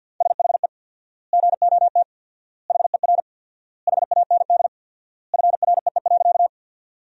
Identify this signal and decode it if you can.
Morse code, heard as a keyed tone, about 46 wpm, 700 Hz. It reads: SHE GOT HER HAND FREE.
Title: 4x Word-Spacing - Sentences from Top 500 Words - 50wpm